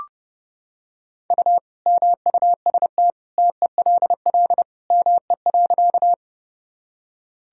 Morse code, heard as a keyed tone, about 30 wpm, 700 Hz. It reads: U MUST TELL ME.
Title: Sentences from Top 200 Words and Abbreviations 30wpm